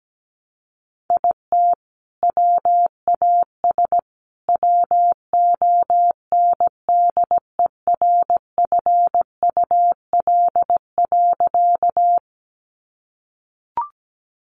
Morse code, heard as a keyed tone, about 17 wpm, 700 Hz. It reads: IT WAS WONDERFUL.
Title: Sentences from Top 1100 Words 17wpm